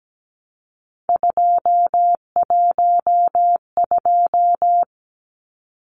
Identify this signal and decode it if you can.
Morse code, heard as a keyed tone, about 17 wpm, 700 Hz. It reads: 212
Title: Numbers - 3-Digit 17wpm